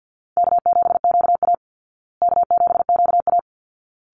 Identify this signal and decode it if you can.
Morse code, heard as a keyed tone, about 50 wpm, 700 Hz. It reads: K7QA K7QA
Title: Easy Call Signs Repeated 2x - 50wpm